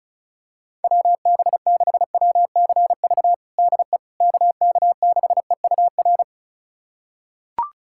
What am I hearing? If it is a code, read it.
Morse code, heard as a keyed tone, about 34 wpm, 700 Hz. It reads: WB6WCV DE KK6EUR
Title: USA Callsign DE Callsign - 6 Characters - 35wpm